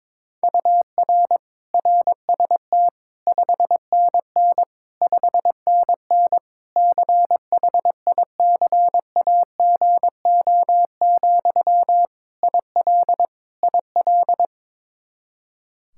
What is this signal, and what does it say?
Morse code, heard as a keyed tone, about 22 wpm, 700 Hz. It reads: UR RST 5NN 5NN CHICAGO, IL IL
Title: K1USN SST QSO Practice 22wpm